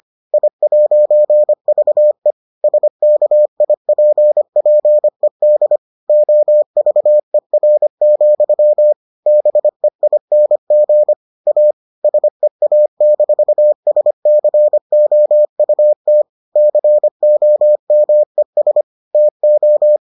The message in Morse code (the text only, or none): I'VE SKIPPED OVER, BEING A SEA-SCOUT COMES TO